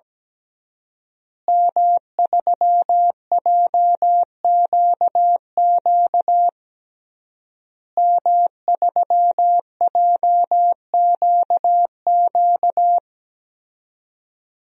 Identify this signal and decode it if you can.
Morse code, heard as a keyed tone, about 17 wpm, 700 Hz. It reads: M3JQQ M3JQQ
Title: DXCC-100 Call Signs - 5 Characters - Repeated 3x - 17wpm